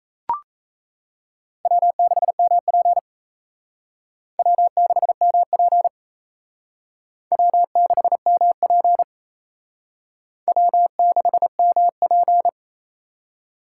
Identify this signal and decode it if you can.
Morse code, heard as a keyed tone, about 36 wpm, 700 Hz. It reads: W6MP W6MP W6MP W6MP